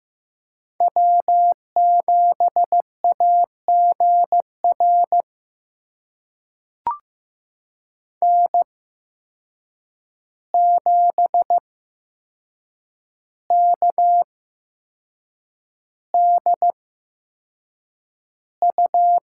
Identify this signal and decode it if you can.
Morse code, heard as a keyed tone, about 15 wpm, spaced as for 6 wpm, 700 Hz. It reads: W7AGR N7KDU
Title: USA Call Signs - 5 Characters - 8x Character Spacing - 15wpm